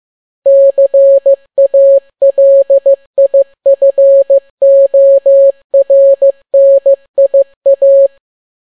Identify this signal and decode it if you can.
Morse code, heard as a keyed tone, about 15 wpm, 550 Hz. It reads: CALIFORNIA
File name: n6tt p1 15C13